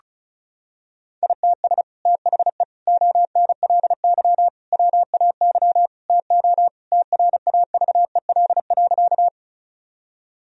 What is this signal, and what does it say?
Morse code, heard as a keyed tone, about 34 wpm, 700 Hz. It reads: ITS THE ONLY WAY TO TRAVEL.
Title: Sentences from Top 1000 Words - Review Part 2 - 35wpm